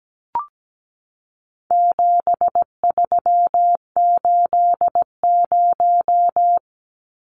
Morse code, heard as a keyed tone, about 17 wpm, 700 Hz. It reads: 7380
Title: Numbers - 4-Digit 17wpm